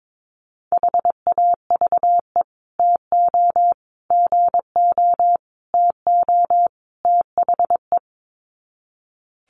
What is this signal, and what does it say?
Morse code, heard as a keyed tone, about 22 wpm, 700 Hz. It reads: HAVE TO GO TO THE
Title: Sets of 5 Words Encore 22wpm